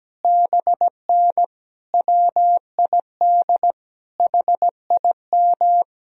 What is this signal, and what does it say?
Morse code, heard as a keyed tone, about 17 wpm, 700 Hz. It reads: BN WID HIM
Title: Sentences from Top 200 Words and Abbreviations 17wpm